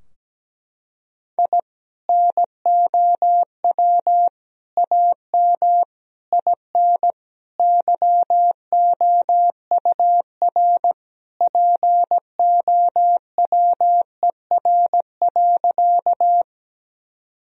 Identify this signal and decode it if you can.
Morse code, heard as a keyed tone, about 17 wpm, 700 Hz. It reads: I NOW AM IN YOUR POWER.
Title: Sentences from Top 1000 Words - Review Part 2 - 17wpm